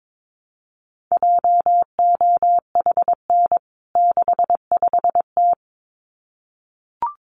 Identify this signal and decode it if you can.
Morse code, heard as a keyed tone, about 22 wpm, 700 Hz. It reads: JOHN 65T
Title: CWT Exchange 22wpm